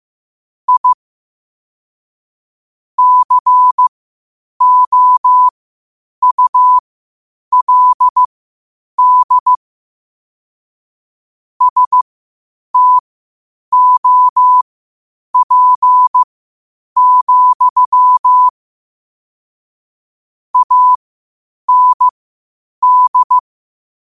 Morse code, heard as a keyed tone, about 15 wpm, spaced as for 8 wpm, 1000 Hz. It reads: I COULD STOP, AND
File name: qotdfiles / 8wpm